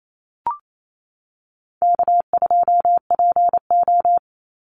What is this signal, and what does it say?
Morse code, heard as a keyed tone, about 28 wpm, 700 Hz. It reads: K2PO
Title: Easy Call Signs - 28wpm